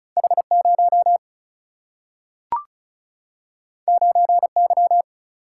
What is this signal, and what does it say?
Morse code, heard as a keyed tone, about 36 wpm, 700 Hz. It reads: H 0 9 Y
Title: Single-Letter Single-Number 35wpm